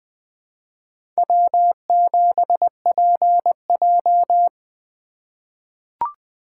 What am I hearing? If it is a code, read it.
Morse code, heard as a keyed tone, about 20 wpm, 700 Hz. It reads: W7PJ
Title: USA Call Signs - 3x Character Spacing - 20wpm